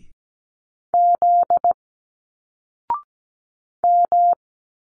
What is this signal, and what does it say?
Morse code, heard as a keyed tone, about 17 wpm, 700 Hz. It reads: Z M
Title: Single Letters 17wpm